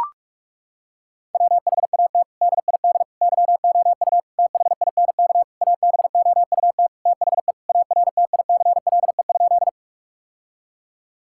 Morse code, heard as a keyed tone, about 42 wpm, 700 Hz. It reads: WHAT DID YOU THINK ABOUT THE ARTICLE?
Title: Sentences from Top 1000 Words - Review Part 5 - 45wpm